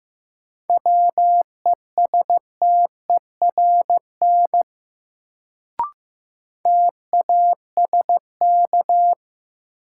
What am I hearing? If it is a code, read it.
Morse code, heard as a keyed tone, about 15 wpm, 700 Hz. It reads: WESTERN TASK
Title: Sentences from Top 1000 Words - Review Part 7 - New Word Drill - 15wpm